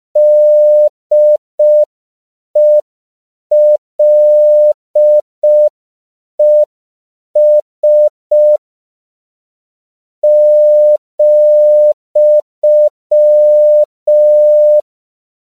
Morse code, heard as a keyed tone, about 5 wpm, 600 Hz. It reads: DELES ,
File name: CW / 5PPM